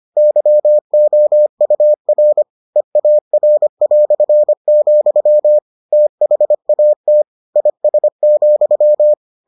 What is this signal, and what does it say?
Morse code, heard as a keyed tone, about 25 wpm, 600 Hz. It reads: YOUR EAR", THAT IS,